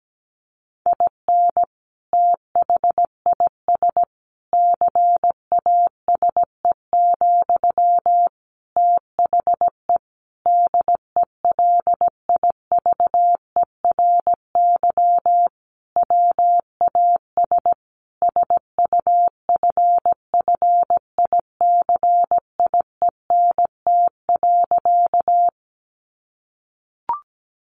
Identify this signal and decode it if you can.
Morse code, heard as a keyed tone, about 17 wpm, 700 Hz. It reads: IN THIS CASE, THE DELIVERY WAS SUFFICIENT.
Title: Sentences from Top 1500 Words 17wpm